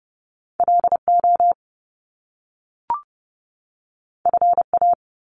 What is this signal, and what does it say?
Morse code, heard as a keyed tone, about 30 wpm, 700 Hz. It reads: LO FA
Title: Top Two Letter Combinations in Words 30wpm